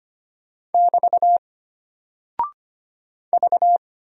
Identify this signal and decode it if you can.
Morse code, heard as a keyed tone, about 25 wpm, 700 Hz. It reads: <BT> V
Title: Single Character (Letter:Number:Punctuation:Prosign) Alt 25wpm